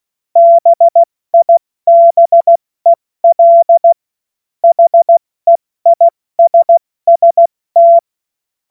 Morse code, heard as a keyed tone, about 16 wpm, spaced as for 14 wpm, 680 Hz. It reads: BIBEL HEISST
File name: Morsekurs / Witze / Witz 21.November 2023 70 F80